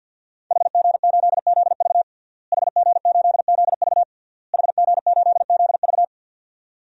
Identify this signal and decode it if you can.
Morse code, heard as a keyed tone, about 47 wpm, 700 Hz. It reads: HG8ZV HG8ZV HG8ZV
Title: DXCC-100 Call Signs - 5 Characters - Repeated 3x - 50wpm